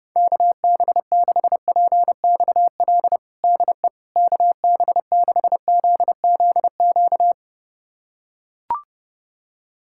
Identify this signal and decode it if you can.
Morse code, heard as a keyed tone, about 30 wpm, 700 Hz. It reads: KB6PXL DE KB6ZZQ
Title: USA Callsign DE Callsign - 6 Characters - 30wpm